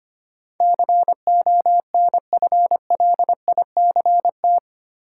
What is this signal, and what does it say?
Morse code, heard as a keyed tone, about 25 wpm, 700 Hz. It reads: CONFLICT
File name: 8 Letter Words - First 3 Letters Given - 25wpm